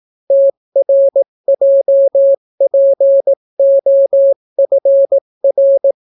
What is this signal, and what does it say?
Morse code, heard as a keyed tone, about 18 wpm, 550 Hz. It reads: TRJPOFR